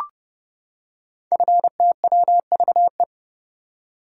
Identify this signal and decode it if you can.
Morse code, heard as a keyed tone, about 30 wpm, 700 Hz. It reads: FTWVE
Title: Code Groups - 5 Letters - 30wpm